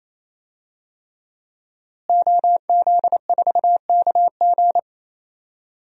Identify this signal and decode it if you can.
Morse code, heard as a keyed tone, about 28 wpm, 700 Hz. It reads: OZ4KG
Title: DXCC-100 Call Signs - 5 Characters - Repeated 3x - 28wpm